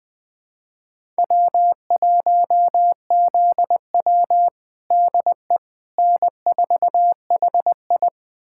Morse code, heard as a keyed tone, about 20 wpm, 700 Hz. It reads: W1ZW DE N4HI
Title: USA Callsign DE Callsign - 4 Characters - 20wpm